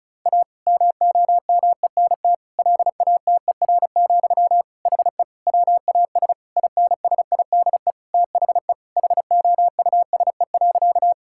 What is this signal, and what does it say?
Morse code, heard as a keyed tone, about 36 wpm, 700 Hz. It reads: A MOMENT LATER, HE WAS INSIDE THE HOUSE.
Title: Sentences from Top 1000 Words - Review Part 6 - 35wpm